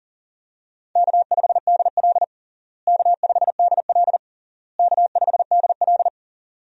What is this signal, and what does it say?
Morse code, heard as a keyed tone, about 40 wpm, 700 Hz. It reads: K5DL K5DL K5DL
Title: USA Call Signs - 4 Characters - Repeated 3x - 40wpm